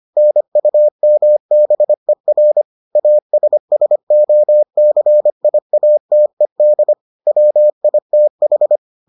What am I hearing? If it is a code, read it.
Morse code, heard as a keyed tone, about 25 wpm, 600 Hz. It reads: NUMBER ASSOCIATED WITH